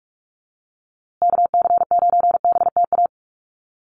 Morse code, heard as a keyed tone, about 44 wpm, 700 Hz. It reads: KC9BTA
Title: USA Call Signs Repeated 2x 45wpm